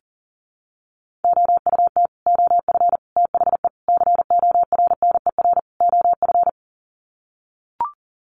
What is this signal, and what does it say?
Morse code, heard as a keyed tone, about 40 wpm, 700 Hz. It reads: OUT OF THE CORNER OF
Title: Sets of 5 Words 40wpm